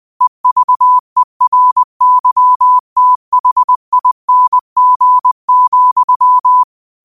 Morse code, heard as a keyed tone, about 20 wpm, 1000 Hz. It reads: EVERYTHING,